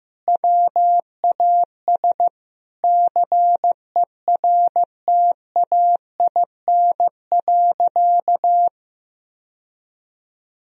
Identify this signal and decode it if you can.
Morse code, heard as a keyed tone, about 15 wpm, 700 Hz. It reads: WAS CERTAIN.
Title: Sentences from Top 1000 Words - Review Part 2 - 15wpm